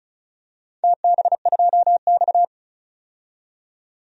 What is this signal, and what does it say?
Morse code, heard as a keyed tone, about 35 wpm, 700 Hz. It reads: TB2X
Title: DXCC-100 Call Signs - 4 Characters - 35wpm